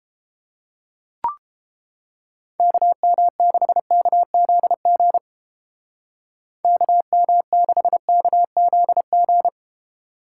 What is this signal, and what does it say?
Morse code, heard as a keyed tone, about 29 wpm, 700 Hz. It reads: KM6KZG KM6KZG